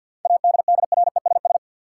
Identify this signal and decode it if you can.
Morse code, heard as a keyed tone, about 50 wpm, 700 Hz. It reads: ADDRESS
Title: Sets of 5 Words Encore - New Word Drill - 50wpm